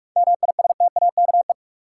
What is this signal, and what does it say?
Morse code, heard as a keyed tone, about 46 wpm, 700 Hz. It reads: MISTAKE
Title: Seven Letter Words - 45wpm